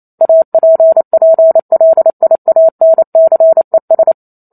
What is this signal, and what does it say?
Morse code, heard as a keyed tone, about 28 wpm, 650 Hz. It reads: APPLIANCES